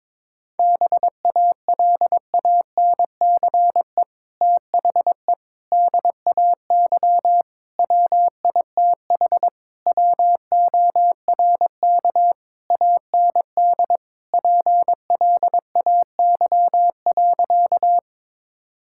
Morse code, heard as a keyed tone, about 22 wpm, 700 Hz. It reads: BALANCE THE DAY WITH WORK AND PLAY.
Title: Sentences from Top 1000 Words - Review Part 6 - 22wpm